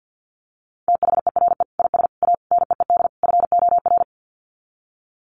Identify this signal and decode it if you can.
Morse code, heard as a keyed tone, about 50 wpm, 700 Hz. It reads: THERE IS A NEED FOR
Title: Sets of 5 Words Encore2 50wpm